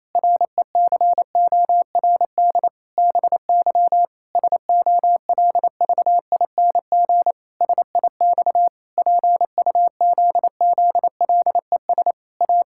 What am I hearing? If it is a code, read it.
Morse code, heard as a keyed tone, about 27 wpm, 700 Hz. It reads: RECORD BY SOLVING SIX PUZZLES A
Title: Pangram Challenge v2 - 28wpm